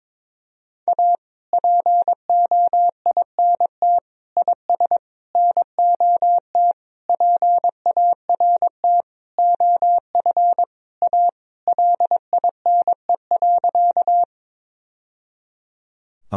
Morse code, heard as a keyed tone, about 23 wpm, 700 Hz. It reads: A POINT IS NOT PART OF A LINE.
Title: Sentences from Top 1000 Words - Review Part 1 - 22wpm